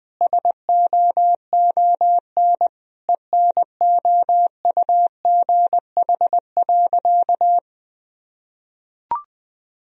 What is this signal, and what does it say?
Morse code, heard as a keyed tone, about 20 wpm, 700 Hz. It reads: SOON ENOUGH.
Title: Sentences from Top 1000 Words - Review Part 4 - 20wpm